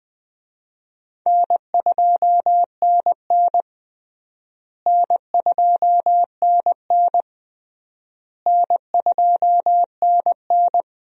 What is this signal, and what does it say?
Morse code, heard as a keyed tone, about 20 wpm, 700 Hz. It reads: N2NN N2NN N2NN
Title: USA Call Signs - 4 Characters - Repeated 3x - 20wpm